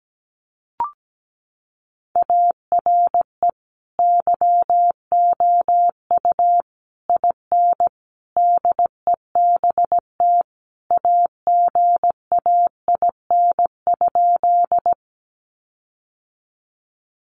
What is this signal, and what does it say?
Morse code, heard as a keyed tone, about 17 wpm, 700 Hz. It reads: ARE YOU IN DEBT AGAIN?
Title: Sentences from Top 1300 Words 17wpm